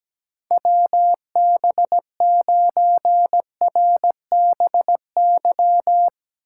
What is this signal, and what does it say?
Morse code, heard as a keyed tone, about 17 wpm, 700 Hz. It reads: WB9RBY
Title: USA Call Signs - 6 Characters - 6x Character Spacing - 17wpm